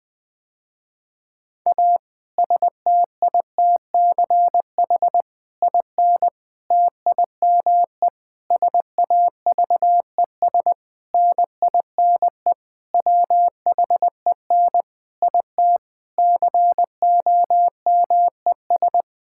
Morse code, heard as a keyed tone, about 20 wpm, 700 Hz. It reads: A STITCH IN TIME SAVES NINE WHEN IT COMES
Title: Difficult Sentences with Hints - 20wpm